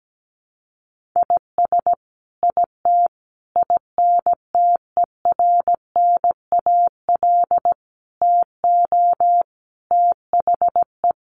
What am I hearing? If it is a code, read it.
Morse code, heard as a keyed tone, about 17 wpm, 700 Hz. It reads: IS IT INTERNAL TO THE
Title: Sentences from Top 1400 Words 17wpm